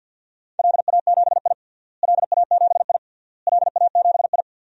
Morse code, heard as a keyed tone, about 51 wpm, 700 Hz. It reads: LA7I LA7I LA7I
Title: DXCC-100 Call Signs - 4 Characters - Repeated 3x - 50wpm